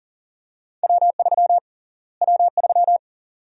Morse code, heard as a keyed tone, about 40 wpm, 700 Hz. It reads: W3 W3
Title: USA Call Sign Prefixes - Repeated 2x - 40wpm